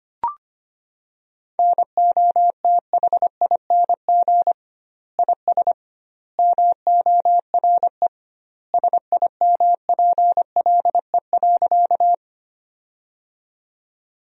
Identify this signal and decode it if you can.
Morse code, heard as a keyed tone, about 25 wpm, 700 Hz. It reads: NOTHING IS MORE SIMPLE.
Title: 2x Word-Spacing - Sentences from Top 500 Words - 25wpm